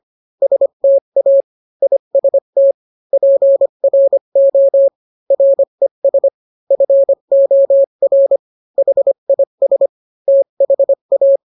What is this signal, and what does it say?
Morse code, heard as a keyed tone, about 25 wpm, 550 Hz. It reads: STA IST PRO RES FOR HIS THA